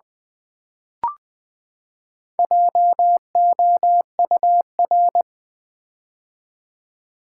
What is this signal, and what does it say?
Morse code, heard as a keyed tone, about 20 wpm, 700 Hz. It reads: JOUR
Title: Build A Word - 4 Letters At A Time - 10-Letter Words - 20wpm